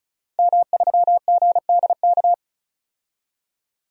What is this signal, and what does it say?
Morse code, heard as a keyed tone, about 36 wpm, 700 Hz. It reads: M3GDK
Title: DXCC-100 Call Signs - 5 Characters - 35wpm